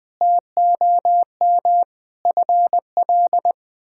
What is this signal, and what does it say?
Morse code, heard as a keyed tone, about 20 wpm, 700 Hz. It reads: TOM FL
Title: K1USN SST QSO Practice - 20wpm